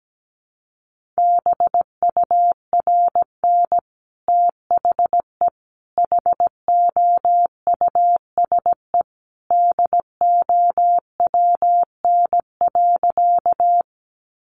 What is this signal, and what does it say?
Morse code, heard as a keyed tone, about 17 wpm, 700 Hz. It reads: BURN THE HOUSE DOWN.
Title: Sentences from Top 1300 Words 17wpm